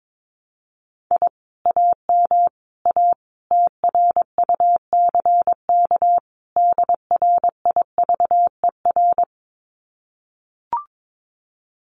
Morse code, heard as a keyed tone, about 22 wpm, 700 Hz. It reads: I AM A TRUCK DRIVER
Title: Common QSO Phrases Encore 22wpm